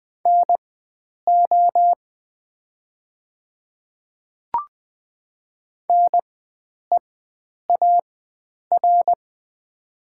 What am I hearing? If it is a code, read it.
Morse code, heard as a keyed tone, about 20 wpm, spaced as for 9 wpm, 690 Hz. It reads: NO NEAR
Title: 4x Character Spacing - Top 300 Words - 20wpm